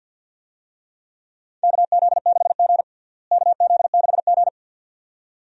Morse code, heard as a keyed tone, about 52 wpm, 700 Hz. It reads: KZ6G KZ6G
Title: USA Call Signs - 4 Characters - Repeated 2x - 50wpm